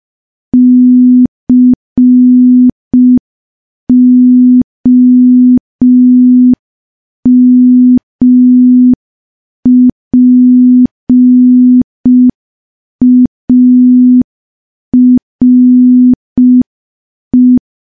Morse code, heard as a keyed tone, about 5 wpm, 250 Hz. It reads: COMPARE
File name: morse / morse-training-05WPM